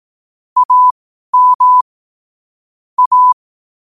Morse code, heard as a keyed tone, about 18 wpm, spaced as for 12 wpm, 1000 Hz. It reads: AM A
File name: 12wpm